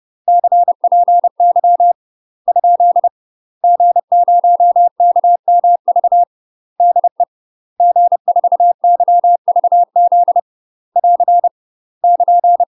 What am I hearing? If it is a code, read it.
Morse code, heard as a keyed tone, about 30 wpm, 700 Hz. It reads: CPY ? G0KMV DE G4YVZ <AR> <KN>
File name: UK MASTER QSOs TRAINING / 30WPM / UK MASTER QSOs 0026